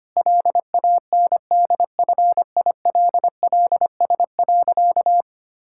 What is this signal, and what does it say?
Morse code, heard as a keyed tone, about 25 wpm, 700 Hz. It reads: LANDFILLS.
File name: practice / 25wpm / Difficult Sentences with Hints - 25wpm